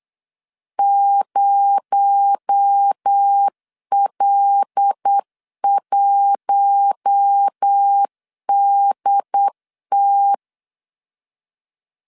Morse code, heard as a keyed tone, about 8 wpm, 800 Hz. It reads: 0L1DT
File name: LEZIONI / 8 YJ45 / 8 Y 8wpm 3245